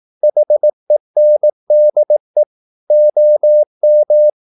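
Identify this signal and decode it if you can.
Morse code, heard as a keyed tone, about 18 wpm, 600 Hz. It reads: HENDE OM
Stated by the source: Slow (QRS) Morse practice at 17 wpm, from CW med Gnister.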